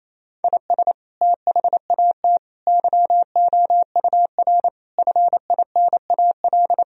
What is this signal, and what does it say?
Morse code, heard as a keyed tone, about 28 wpm, 700 Hz. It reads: IS THAT YOUR FINAL
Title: Sentences from Top 1000 Words - Review Part 2 - 28wpm